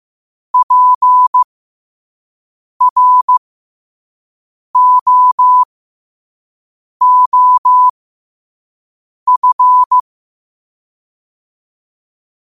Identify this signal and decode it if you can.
Morse code, heard as a keyed tone, about 15 wpm, spaced as for 5 wpm, 1000 Hz. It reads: PROOF
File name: qotdfiles / 5wpm